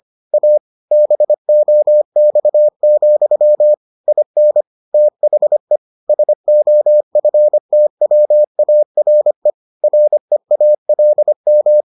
Morse code, heard as a keyed tone, about 25 wpm, 600 Hz. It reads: A BOX, IN THE SOFTWARE REALM